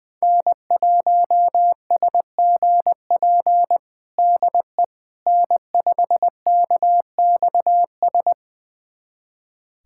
Morse code, heard as a keyed tone, about 20 wpm, 700 Hz. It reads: N1SGP DE N5KXS
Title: USA Callsign DE Callsign - 5 Characters - 20wpm